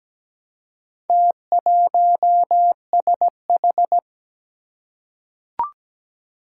Morse code, heard as a keyed tone, about 17 wpm, 700 Hz. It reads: T1SH
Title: Four Challenging Letter-Number Combinations 17wpm